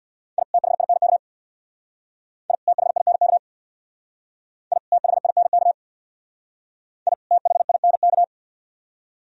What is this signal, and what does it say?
Morse code, heard as a keyed tone, about 60 wpm, 700 Hz. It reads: I THINK I THINK I THINK I THINK
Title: Speed-Racing - Sets of 2 Words 50wpm